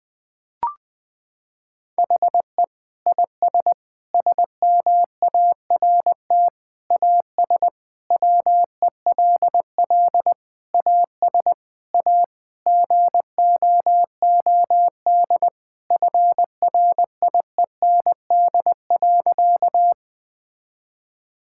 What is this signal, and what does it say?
Morse code, heard as a keyed tone, about 20 wpm, 700 Hz. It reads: HE IS SMART AS WELL AS A GOOD FRIEND.
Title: Sentences with 4-Word Phrases - 20wpm